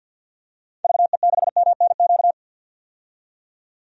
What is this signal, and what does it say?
Morse code, heard as a keyed tone, about 45 wpm, 700 Hz. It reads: VE6MNQ
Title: DXCC-100 Call Signs - 6 Characters - 50wpm